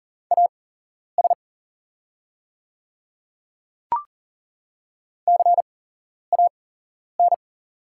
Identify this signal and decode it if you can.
Morse code, heard as a keyed tone, about 38 wpm, spaced as for 11 wpm, 700 Hz. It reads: AS CAN
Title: 8x Character Spacing - Top 100 Words - 40wpm